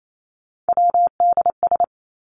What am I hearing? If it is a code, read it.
Morse code, heard as a keyed tone, about 28 wpm, 700 Hz. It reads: WDS
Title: Abbreviations 28wpm